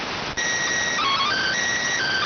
SSTV (slow-scan television), PD 290.